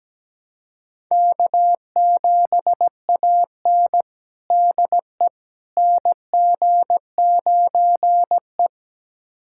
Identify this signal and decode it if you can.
Morse code, heard as a keyed tone, about 17 wpm, 700 Hz. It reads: K7AN DE NG9E
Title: USA Callsign DE Callsign - 4 Characters - 17wpm